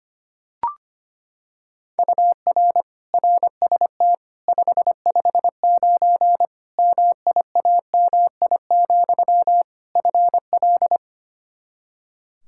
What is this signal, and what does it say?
Morse code, heard as a keyed tone, about 25 wpm, 700 Hz. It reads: UR RST 559 MIAMI, FL
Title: K1USN SST QSO Practice 25wpm